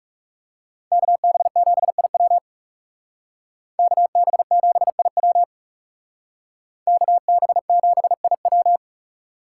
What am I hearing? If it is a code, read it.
Morse code, heard as a keyed tone, about 36 wpm, 700 Hz. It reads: KB7IW KB7IW KB7IW
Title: Speed Racing - USA 2x2 Call Signs 30wpm